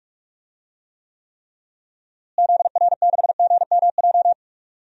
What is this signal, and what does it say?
Morse code, heard as a keyed tone, about 46 wpm, 700 Hz. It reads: ZR6GMJ
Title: DXCC-100 Call Signs - 6 Characters - Repeated 3x - 45wpm